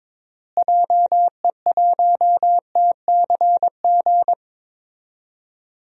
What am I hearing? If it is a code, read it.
Morse code, heard as a keyed tone, about 22 wpm, 700 Hz. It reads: JE1TCG
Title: DXCC-100 Call Signs - 6 Characters - Repeated 3x - 22wpm